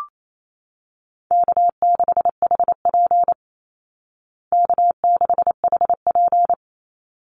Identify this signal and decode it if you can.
Morse code, heard as a keyed tone, about 28 wpm, 700 Hz. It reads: K6HP K6HP
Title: Easy Call Signs Repeated 2x - 28wpm